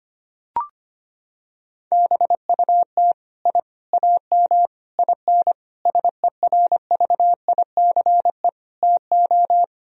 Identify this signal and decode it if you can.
Morse code, heard as a keyed tone, about 25 wpm, 700 Hz. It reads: BUT I AM IN SERVICE TO
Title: Sentences from Top 1000 Words - Review Part 4 - 25wpm